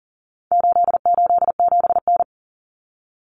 Morse code, heard as a keyed tone, about 40 wpm, 700 Hz. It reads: 887N
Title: Common Cut Numbers - 4-Digit 40wpm